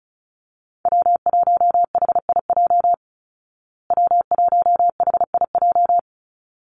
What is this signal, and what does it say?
Morse code, heard as a keyed tone, about 34 wpm, 700 Hz. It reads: W1HIJ W1HIJ
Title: Call Signs Repeated 3x - 35wpm